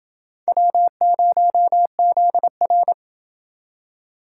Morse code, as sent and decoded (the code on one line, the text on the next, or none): .-- ----- --.. .-.
W0ZR